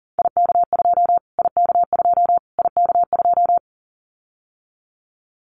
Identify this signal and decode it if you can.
Morse code, heard as a keyed tone, about 40 wpm, 700 Hz. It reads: IK2 IK2 IK2
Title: Callsign Prefixes for Top 25 DXCC - Repeated 3x - 40wpm